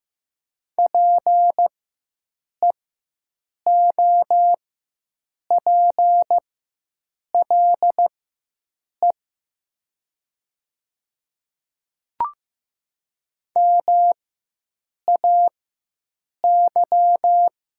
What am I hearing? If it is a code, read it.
Morse code, heard as a keyed tone, about 15 wpm, spaced as for 7 wpm, 700 Hz. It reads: PEOPLE MAY
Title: 4x Character Spacing - Top 100 Words - 15wpm